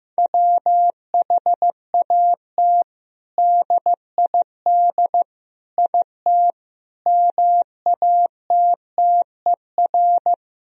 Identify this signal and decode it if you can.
Morse code, heard as a keyed tone, about 15 wpm, 700 Hz. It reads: WHAT DID IT MATTER
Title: Sentences from Top 1000 Words - Review Part 3 - 15wpm